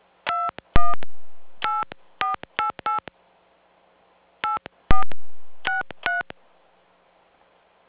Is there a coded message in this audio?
Yes: DTMF from 0.3 to 6.2 s, heard as telephone touch tones, tones of about 150 ms. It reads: 218488 8463